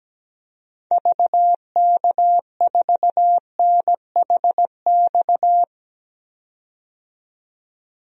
Morse code, heard as a keyed tone, about 17 wpm, 700 Hz. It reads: VK4NHX
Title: DXCC-100 Call Signs - 6 Characters - 17wpm